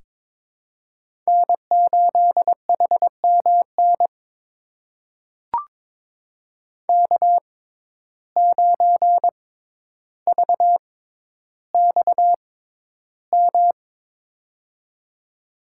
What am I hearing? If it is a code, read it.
Morse code, heard as a keyed tone, about 22 wpm, spaced as for 11 wpm, 700 Hz. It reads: N8HMN K9VXM
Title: USA Call Signs - 5 Characters - 6x Character Spacing - 22wpm